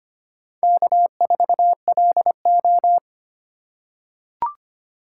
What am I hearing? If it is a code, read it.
Morse code, heard as a keyed tone, about 25 wpm, 700 Hz. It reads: K4LO
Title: USA Call Signs - 2x Character Spacing - 25wpm